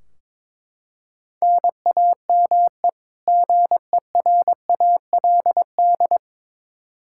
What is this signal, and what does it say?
Morse code, heard as a keyed tone, about 22 wpm, 700 Hz. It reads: NAME GERALD
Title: Practice QSO - Exchange 1 - Beginning - 22wpm